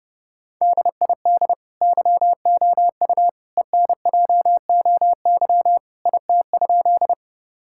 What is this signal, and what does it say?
Morse code, heard as a keyed tone, about 32 wpm, 700 Hz. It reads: DID YOU ENJOY IT?
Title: Sentences from Top 1000 Words - Review Part 5 - 30wpm